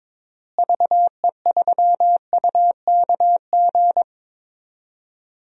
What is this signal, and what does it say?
Morse code, heard as a keyed tone, about 22 wpm, 700 Hz. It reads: VE3UKG
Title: DXCC-100 Call Signs - 6 Characters - Repeated 3x - 22wpm